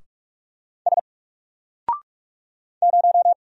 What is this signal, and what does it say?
Morse code, heard as a keyed tone, about 46 wpm, 700 Hz. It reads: S 0
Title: Single Character (Letter:Number:Punctuation:Prosign) 45wpm